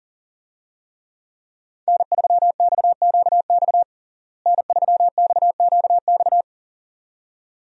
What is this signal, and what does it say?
Morse code, heard as a keyed tone, about 40 wpm, 700 Hz. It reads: N3XQX N3XQX
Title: USA Call Signs - 5 Characters - Repeated 2x - 40wpm